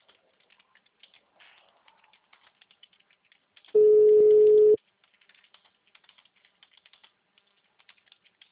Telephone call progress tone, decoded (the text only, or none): ringback tone